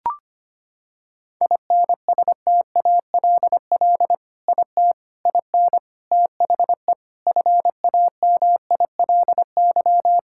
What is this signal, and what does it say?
Morse code, heard as a keyed tone, about 25 wpm, 700 Hz. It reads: INSTALL IT IN THE FAMILY
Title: Sentences from Top 1600 Words 25wpm